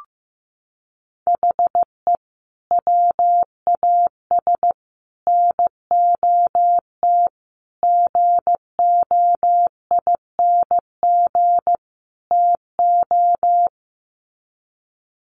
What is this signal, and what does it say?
Morse code, heard as a keyed tone, about 15 wpm, 700 Hz. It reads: HE WAS NOT GOING TO
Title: Sets of 5 Words 15wpm